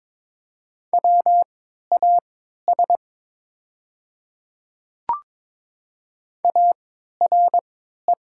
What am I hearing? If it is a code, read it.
Morse code, heard as a keyed tone, about 22 wpm, spaced as for 13 wpm, 700 Hz. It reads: WAS ARE
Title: 3x Character Spacing - Top 200 Words - 22wpm